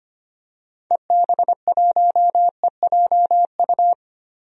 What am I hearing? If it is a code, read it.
Morse code, heard as a keyed tone, about 25 wpm, 700 Hz. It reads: EB1EJU
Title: DXCC-100 Call Signs - 6 Characters - 25wpm